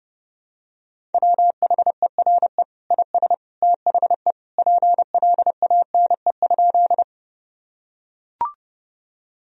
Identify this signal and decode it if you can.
Morse code, heard as a keyed tone, about 32 wpm, 700 Hz. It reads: WHERE IS THE PLANE?
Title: Sentences from Top 1000 Words - Review Part 3 - 30wpm